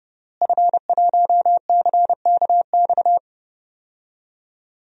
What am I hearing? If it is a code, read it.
Morse code, heard as a keyed tone, about 32 wpm, 700 Hz. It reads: F1CKX
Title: DXCC-100 Call Signs - 5 Characters - Repeated 2x - 30wpm